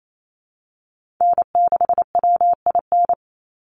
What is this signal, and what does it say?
Morse code, heard as a keyed tone, about 28 wpm, 700 Hz. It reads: N6WIN
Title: Call Signs Repeated 2x - 28wpm